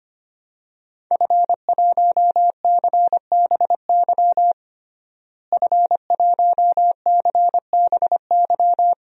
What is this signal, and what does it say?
Morse code, heard as a keyed tone, about 25 wpm, 700 Hz. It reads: F1CBY F1CBY
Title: DXCC-100 Call Signs - 5 Characters - Repeated 3x - 25wpm